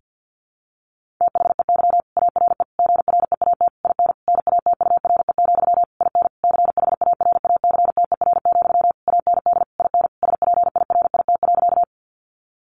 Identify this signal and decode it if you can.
Morse code, heard as a keyed tone, about 50 wpm, 700 Hz. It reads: THEY ARE GREAT IN NATURE, IN CHARACTER, AND IN SPIRIT.
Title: Sentences from Top 1400 Words 50wpm